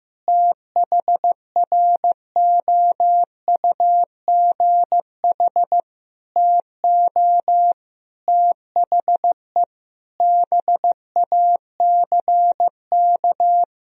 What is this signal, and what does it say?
Morse code, heard as a keyed tone, about 15 wpm, 700 Hz. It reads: THROUGH TO THE BACK
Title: Sentences from Top 1000 Words - Review Part 1 - 15wpm